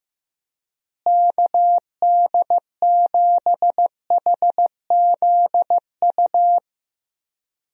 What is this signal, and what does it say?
Morse code, heard as a keyed tone, about 15 wpm, 700 Hz. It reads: KD7HZU
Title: USA Call Signs - 6 Characters - Repeated 3x - 15wpm